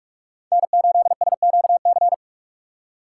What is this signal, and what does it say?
Morse code, heard as a keyed tone, about 46 wpm, 700 Hz. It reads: N8SQC
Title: USA Call Signs - 2x Character Spacing - 45wpm